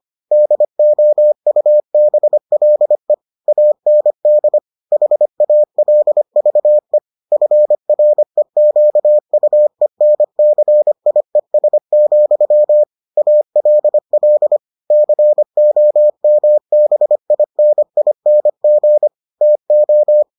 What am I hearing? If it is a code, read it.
Morse code, heard as a keyed tone, about 25 wpm, 600 Hz. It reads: DOUBLE AND HALVE FREQUENCIES, ALL COMBINING TO